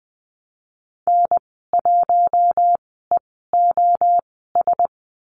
Morse code, heard as a keyed tone, about 20 wpm, spaced as for 15 wpm, 700 Hz. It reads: N1EOS
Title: 088 Lesson 15g - Number 5 - Call Signs 20wpm